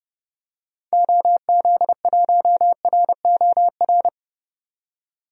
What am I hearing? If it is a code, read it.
Morse code, heard as a keyed tone, about 30 wpm, 700 Hz. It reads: OZ1ROR
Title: DXCC-100 Call Signs - 6 Characters - 30wpm